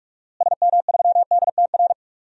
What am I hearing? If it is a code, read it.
Morse code, heard as a keyed tone, about 44 wpm, 700 Hz. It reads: SM3DTR
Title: DXCC-100 Call Signs - 6 Characters - Repeated 3x - 45wpm